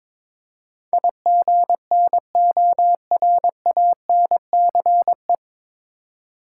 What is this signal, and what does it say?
Morse code, heard as a keyed tone, about 22 wpm, 700 Hz. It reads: IGNORANCE
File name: 9 Letter Words - First 3 Letters Given - 22wpm